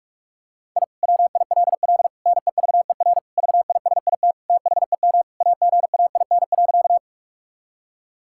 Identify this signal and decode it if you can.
Morse code, heard as a keyed tone, about 46 wpm, 700 Hz. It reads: I WILL NEVER VISIT THEM AGAIN.
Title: Sentences from Top 1000 Words - Review Part 6 - 45wpm